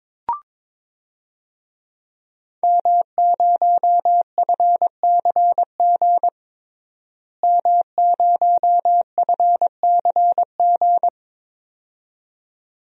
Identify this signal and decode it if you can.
Morse code, heard as a keyed tone, about 22 wpm, 700 Hz. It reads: M0FCG M0FCG
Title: DXCC-100 Call Signs - 5 Characters - Repeated 2x - 22wpm